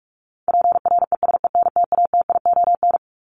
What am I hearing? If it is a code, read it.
Morse code, heard as a keyed tone, about 44 wpm, 700 Hz. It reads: PRESENTATION
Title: Top 1501-1750 Words 45wpm